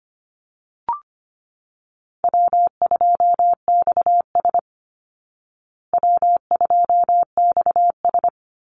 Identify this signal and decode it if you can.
Morse code, heard as a keyed tone, about 25 wpm, 700 Hz. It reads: W2XS W2XS
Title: Easy Call Signs Repeated 2x - 25wpm